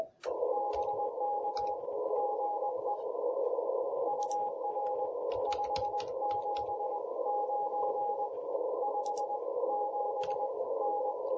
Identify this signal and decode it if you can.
Morse code, heard as a keyed tone, about 20 wpm, 820 Hz, with radio noise. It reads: BK GE UR 579 57N
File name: jh8cbh / 2021cw